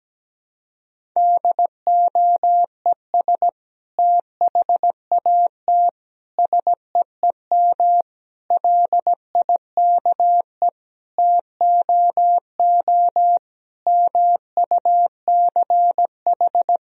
Morse code, heard as a keyed tone, about 17 wpm, 700 Hz. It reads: DOES THAT SEEM LIKE TOO MUCH
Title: Sentences from Top 1000 Words - Review Part 1 - 17wpm